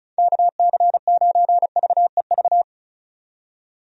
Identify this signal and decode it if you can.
Morse code, heard as a keyed tone, about 34 wpm, 700 Hz. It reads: KC9VEV
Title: USA Call Signs - 6 Characters - 35wpm